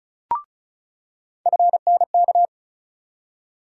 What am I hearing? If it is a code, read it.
Morse code, heard as a keyed tone, about 36 wpm, 700 Hz. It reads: FNK
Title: USA Call Sign Postfixes - 3 Letters 35wpm